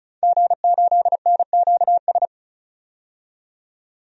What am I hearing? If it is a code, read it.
Morse code, heard as a keyed tone, about 34 wpm, 700 Hz. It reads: G8NQS
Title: DXCC-100 Call Signs - 5 Characters - 35wpm